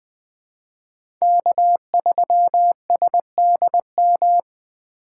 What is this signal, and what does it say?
Morse code, heard as a keyed tone, about 20 wpm, 700 Hz. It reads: K3SDM
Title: 2023 POTA Hunter Call Signs - 20wpm